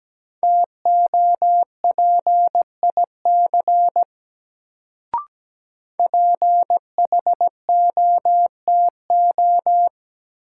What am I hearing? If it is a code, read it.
Morse code, heard as a keyed tone, about 17 wpm, 700 Hz. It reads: TOPIC PHOTO
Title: Sentences from Top 1200 Words - New Word Drill - 17wpm